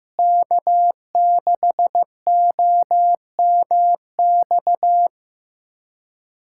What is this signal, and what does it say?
Morse code, heard as a keyed tone, about 15 wpm, 700 Hz. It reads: K6OMX
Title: USA Callsign DE Callsign - 5 Characters - 15wpm